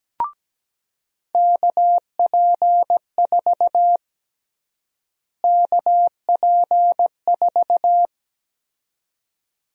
Morse code, heard as a keyed tone, about 17 wpm, 700 Hz. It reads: KP4 KP4
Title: USA Call Sign Prefixes - Repeated 2x - 17wpm